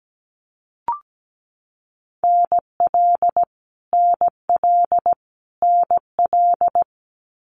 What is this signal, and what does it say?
Morse code, heard as a keyed tone, about 17 wpm, 700 Hz. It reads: NL NL NL
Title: Canadian Provinces and Territories - Repeated 3x - 17wpm